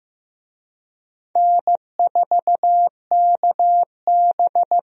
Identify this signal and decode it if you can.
Morse code, heard as a keyed tone, about 16 wpm, 700 Hz. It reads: N4KB
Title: USA Call Signs - 4 Characters - Repeated 2x - 15wpm